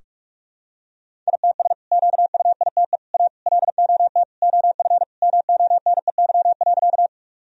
Morse code, heard as a keyed tone, about 46 wpm, 700 Hz. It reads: ITS QUITE A LOT OF MONEY.
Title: Sentences from Top 1000 Words - Review Part 3 - 45wpm